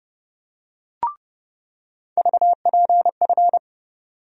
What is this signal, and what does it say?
Morse code, heard as a keyed tone, about 30 wpm, 700 Hz. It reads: VPF
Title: USA Call Sign Postfixes - 3 Letters 30wpm